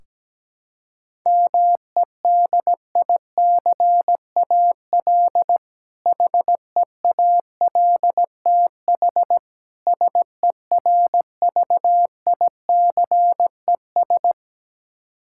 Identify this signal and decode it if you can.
Morse code, heard as a keyed tone, about 17 wpm, 700 Hz. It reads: MEDICAL HEALTH SERVICES
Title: K1USN SST QSO Practice 17wpm